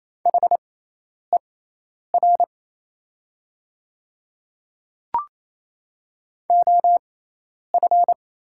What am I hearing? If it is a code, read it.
Morse code, heard as a keyed tone, about 28 wpm, spaced as for 10 wpm, 690 Hz. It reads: HER OF
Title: 6x Character Spacing - Top 100 Words - 28wpm